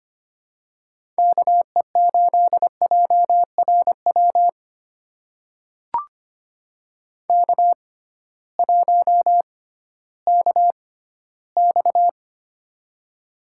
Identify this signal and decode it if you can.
Morse code, heard as a keyed tone, about 25 wpm, spaced as for 15 wpm, 700 Hz. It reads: KE8JRW K1KX
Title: USA Call Signs - 6x Character Spacing - 25wpm